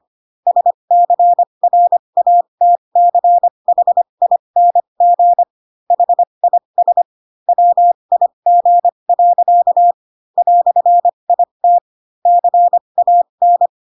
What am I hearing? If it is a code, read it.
Morse code, heard as a keyed tone, about 25 wpm, spaced as for 22 wpm, 700 Hz. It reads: SCRATCHING HIS WIG. "IT CAN